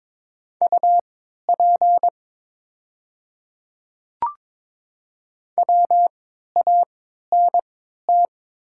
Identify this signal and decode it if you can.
Morse code, heard as a keyed tone, about 22 wpm, spaced as for 12 wpm, 690 Hz. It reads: UP WANT
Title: 3x Character Spacing - Top 200 Words - 22wpm